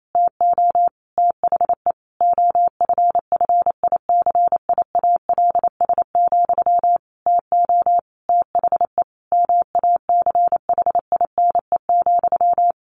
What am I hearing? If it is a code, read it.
Morse code, heard as a keyed tone, about 28 wpm, 700 Hz. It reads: TO THE OFFICIALS, TO THE MACHINE,
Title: A Sound Of Thunder 28wpm